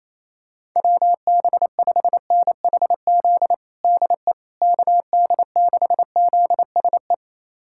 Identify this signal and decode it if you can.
Morse code, heard as a keyed tone, about 28 wpm, 700 Hz. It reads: WB5NHZ DE KD6ZSE
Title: USA Callsign DE Callsign 28wpm